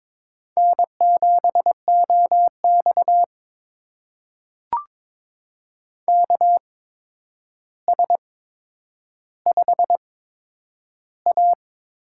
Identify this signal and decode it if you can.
Morse code, heard as a keyed tone, about 22 wpm, spaced as for 9 wpm, 700 Hz. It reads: N7OX KS5A
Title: USA Call Signs - 4 Characters - 8x Character Spacing - 22wpm